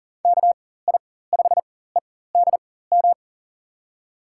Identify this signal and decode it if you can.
Morse code, heard as a keyed tone, about 39 wpm, spaced as for 19 wpm, 700 Hz. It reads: KI5EDM
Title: USA Call Signs - 6 Characters - 4x Character Spacing - 40wpm